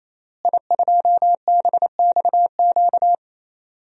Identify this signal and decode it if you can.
Morse code, heard as a keyed tone, about 27 wpm, 700 Hz. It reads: I2BXQ
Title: DXCC-100 Call Signs - 5 Characters - Repeated 3x - 28wpm